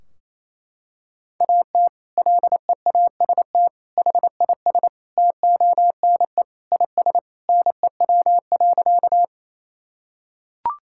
Morse code, heard as a keyed tone, about 28 wpm, 700 Hz. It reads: AT LEAST HIS TONE IS NEW.
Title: Sentences from Top 1600 Words 28wpm